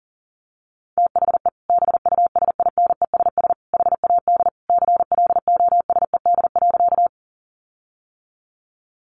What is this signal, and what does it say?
Morse code, heard as a keyed tone, about 40 wpm, 700 Hz. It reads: THE BUSINESS HAD CLOSED.
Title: Sentences from Top 1400 Words 40wpm